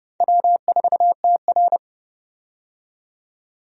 Morse code, heard as a keyed tone, about 30 wpm, 700 Hz. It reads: W4TR
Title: USA Call Signs - 4 Characters - 30wpm